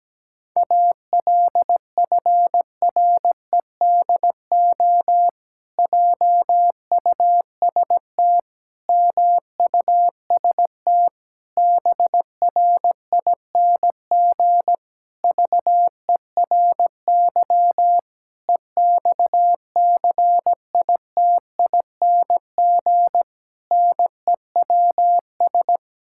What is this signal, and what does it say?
Morse code, heard as a keyed tone, about 17 wpm, 700 Hz. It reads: ALFREDO JUST MUST BRING VERY EXCITING NEWS
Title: Pangram Challenge v2 - 17wpm